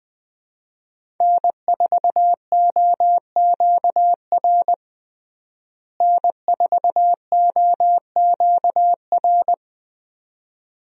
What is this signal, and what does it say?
Morse code, heard as a keyed tone, about 20 wpm, 700 Hz. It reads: N4OQR N4OQR
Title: USA Call Signs - 5 Characters - Repeated 2x - 20wpm